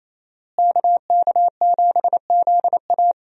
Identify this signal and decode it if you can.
Morse code, heard as a keyed tone, about 27 wpm, 700 Hz. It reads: KK7ZA
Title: USA Call Signs - 5 Characters - Repeated 3x - 28wpm